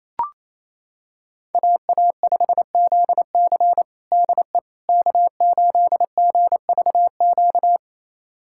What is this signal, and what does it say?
Morse code, heard as a keyed tone, about 28 wpm, 700 Hz. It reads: AA5ZC DE K8GVQ
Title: USA Callsign DE Callsign - 5 Characters - 28wpm